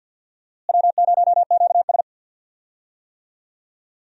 Morse code, heard as a keyed tone, about 52 wpm, 700 Hz. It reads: W0QS
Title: USA Call Signs - 4 Characters - Repeated 3x - 50wpm